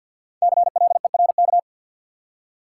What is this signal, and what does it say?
Morse code, heard as a keyed tone, about 53 wpm, 700 Hz. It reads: CLERK
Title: Five Letter Words - 50wpm